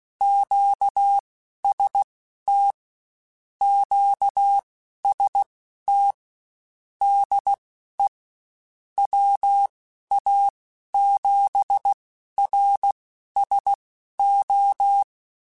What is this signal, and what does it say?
Morse code, heard as a keyed tone, about 16 wpm, spaced as for 11 wpm, 800 Hz. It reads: QST QST DE WA7RSO